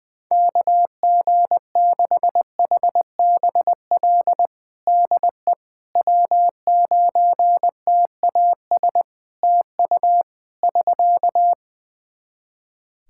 Morse code, heard as a keyed tone, about 20 wpm, 700 Hz. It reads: KG6HBL DE W9TAS TU <SK>
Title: Practice QSO - Exchange 4 - Ending - 20wpm